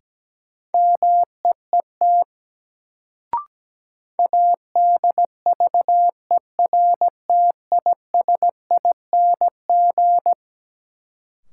Morse code, heard as a keyed tone, about 17 wpm, 700 Hz. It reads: MEET ADVERTISING
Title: Sentences from Top 1000 Words - Review Part 6 - New Word Drill - 17wpm